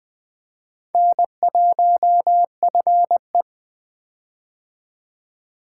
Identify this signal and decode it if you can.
Morse code, heard as a keyed tone, about 21 wpm, 700 Hz. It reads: N1FE
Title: K1USN SST QSO Practice - 20wpm